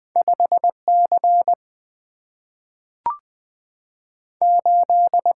Morse code, heard as a keyed tone, about 20 wpm, 700 Hz. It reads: 5 C 8
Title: Single-Letter Single-Number 20wpm